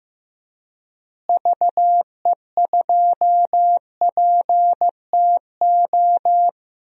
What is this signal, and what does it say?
Morse code, heard as a keyed tone, about 15 wpm, 700 Hz. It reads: VE2PTO
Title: DXCC-100 Call Signs - 6 Characters - Repeated 3x - 15wpm